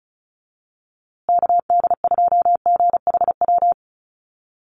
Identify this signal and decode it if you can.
Morse code, heard as a keyed tone, about 34 wpm, 700 Hz. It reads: KD2GHW
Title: USA Call Signs Repeated 3x 35wpm